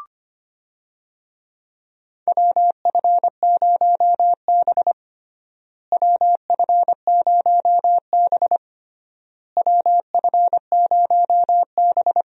Morse code, heard as a keyed tone, about 25 wpm, 700 Hz. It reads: WF0B WF0B WF0B
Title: USA Call Signs - 4 Characters - Repeated 3x - 25wpm